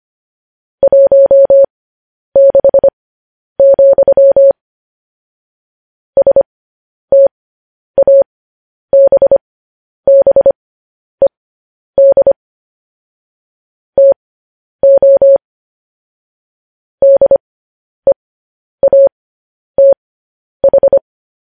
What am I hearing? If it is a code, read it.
Morse code, heard as a keyed tone, about 25 wpm, spaced as for 10 wpm, 550 Hz. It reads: 16, STABBED TO DEATH
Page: news headlines in morse code at 10 wpm